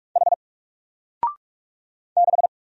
Morse code, heard as a keyed tone, about 44 wpm, 700 Hz. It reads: H 6
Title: Single Letter-Number Alt 45wpm